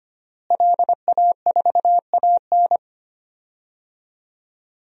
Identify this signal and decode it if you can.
Morse code, heard as a keyed tone, about 25 wpm, 700 Hz. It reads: LA4AN
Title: DXCC-100 Call Signs - 5 Characters - Repeated 2x - 25wpm